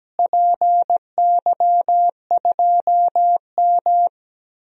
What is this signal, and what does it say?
Morse code, heard as a keyed tone, about 17 wpm, 700 Hz. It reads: PY2M
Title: DXCC-100 Call Signs - 4 Characters - Repeated 2x - 17wpm